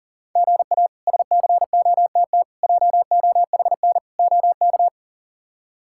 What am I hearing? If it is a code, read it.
Morse code, heard as a keyed tone, about 40 wpm, 700 Hz. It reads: GA SCOTT JOHN OK
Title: K1USN SST QSO Practice - 40wpm